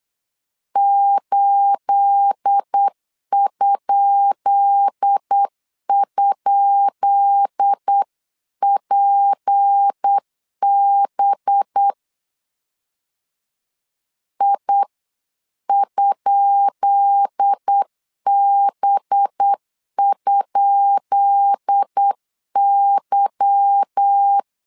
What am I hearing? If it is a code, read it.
Morse code, heard as a keyed tone, about 8 wpm, 800 Hz. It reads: 8??PB I?B?Y
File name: LEZIONI / 10 SIGNS / 10B INT 8wpm 3534